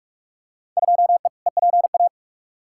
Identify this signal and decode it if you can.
Morse code, heard as a keyed tone, about 46 wpm, 700 Hz. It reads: 2E EPA
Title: Field Day Exchanges Encore - 45wpm